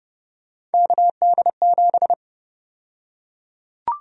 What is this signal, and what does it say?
Morse code, heard as a keyed tone, about 30 wpm, 700 Hz. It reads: KD7
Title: USA Call Sign Prefixes - 30wpm